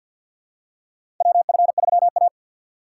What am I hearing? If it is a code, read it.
Morse code, heard as a keyed tone, about 53 wpm, 700 Hz. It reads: WF3A
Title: USA Call Signs - 4 Characters - 6x Character Spacing - 50wpm